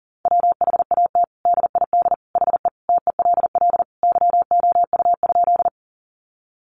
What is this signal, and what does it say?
Morse code, heard as a keyed tone, about 40 wpm, 700 Hz. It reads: WHAT DID HE TELL YOU?
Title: Sentences from Top 200 Words 40wpm